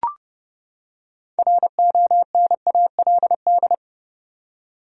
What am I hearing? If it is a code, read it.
Morse code, heard as a keyed tone, about 31 wpm, 700 Hz. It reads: RONALD
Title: USA Common Names - 30wpm